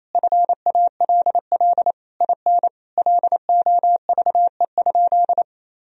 Morse code, heard as a keyed tone, about 28 wpm, 700 Hz. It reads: FALL IN LOVE?
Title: Sentences from Top 1000 Words - Review Part 2 - 28wpm